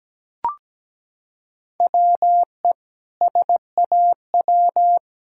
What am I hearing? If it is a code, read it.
Morse code, heard as a keyed tone, about 17 wpm, 700 Hz. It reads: WE SAW
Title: Sentences from Top 1000 Words - Review Part 6 - 17wpm